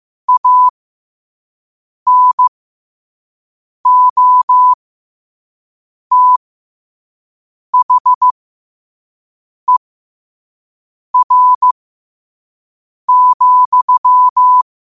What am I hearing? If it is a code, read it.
Morse code, heard as a keyed tone, about 15 wpm, spaced as for 5 wpm, 1000 Hz. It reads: ANOTHER,
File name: qotdfiles / 5wpm